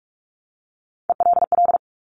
Morse code, heard as a keyed tone, about 46 wpm, 700 Hz. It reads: ELL
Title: Top Three Letter Combinations in Words 45wpm